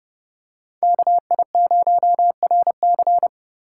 Morse code, heard as a keyed tone, about 29 wpm, 700 Hz. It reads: KI0RC